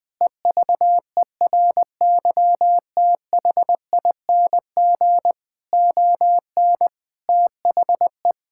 Morse code, heard as a keyed tone, about 20 wpm, 700 Hz. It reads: EVERYTHING ON THE
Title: All-Age Appropriate Jokes - 20wpm